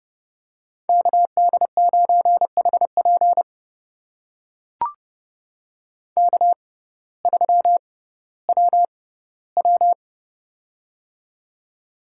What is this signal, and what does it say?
Morse code, heard as a keyed tone, about 31 wpm, spaced as for 17 wpm, 700 Hz. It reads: KD9HP K3WW
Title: USA Call Signs - 6x Character Spacing - 30wpm